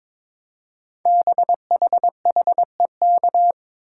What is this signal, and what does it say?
Morse code, heard as a keyed tone, about 23 wpm, 700 Hz. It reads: BHHEK
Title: Code Groups - 5 Letters - 22wpm